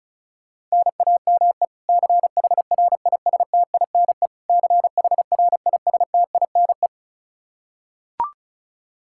Morse code, heard as a keyed tone, about 36 wpm, 700 Hz. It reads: NAME CHRISTINE CHRISTINE
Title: Practice QSO - Exchange 1 - Beginning - 35wpm